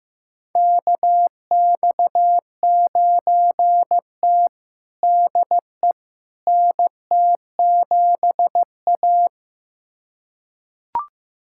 Morse code, heard as a keyed tone, about 15 wpm, 700 Hz. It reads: KX9T DE NT7A
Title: USA Callsign DE Callsign - 4 Characters - 15wpm